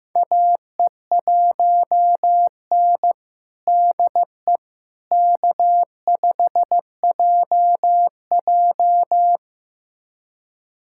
Morse code, heard as a keyed tone, about 15 wpm, 700 Hz. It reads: AE1N DE K5JJ
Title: USA Callsign DE Callsign - 4 Characters - 15wpm